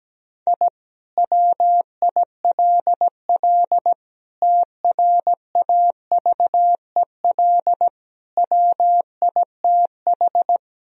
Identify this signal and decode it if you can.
Morse code, heard as a keyed tone, about 17 wpm, 700 Hz. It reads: I WILL TRAVEL WITH
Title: Sentences from Top 1000 Words - Review Part 2 - 17wpm